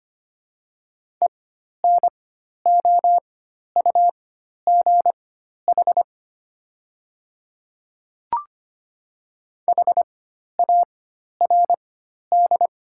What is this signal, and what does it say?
Morse code, heard as a keyed tone, about 25 wpm, spaced as for 12 wpm, 700 Hz. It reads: ENOUGH HARD
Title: 4x Character Spacing - Top 500 Words - 25wpm